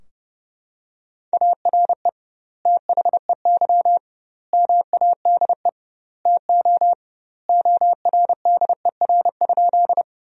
Morse code, heard as a keyed tone, about 32 wpm, 700 Hz. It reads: ARE THEY MADE TO ORDER?
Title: 2x Word-Spacing - Sentences from Top 400 Words - 30wpm